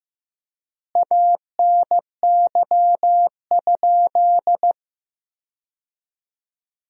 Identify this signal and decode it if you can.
Morse code, heard as a keyed tone, about 15 wpm, 700 Hz. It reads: ANY?
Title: 3x Word-Spacing - Sentences from Top 300 Words - 15wpm